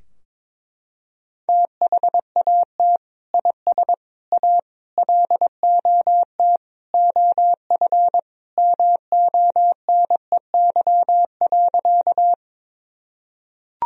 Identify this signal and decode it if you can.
Morse code, heard as a keyed tone, about 22 wpm, 700 Hz. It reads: THAT IS A LOT OF MONEY.
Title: Sentences from Top 1000 Words - Review Part 3 - 22wpm